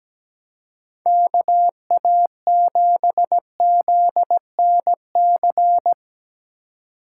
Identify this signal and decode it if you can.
Morse code, heard as a keyed tone, about 19 wpm, 700 Hz. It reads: KA7ZNC
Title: Speed Racing - USA 2x3 Call Signs 17wpm